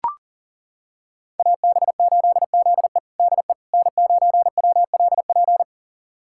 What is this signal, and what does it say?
Morse code, heard as a keyed tone, about 40 wpm, 700 Hz. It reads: AB8ZE DE N9WLP
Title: USA Callsign DE Callsign - 5 Characters - 40wpm